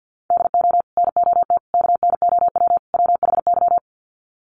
Morse code, heard as a keyed tone, about 50 wpm, 700 Hz. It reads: DO NOT KNOW WHY
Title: Sets of 4 Words 50wpm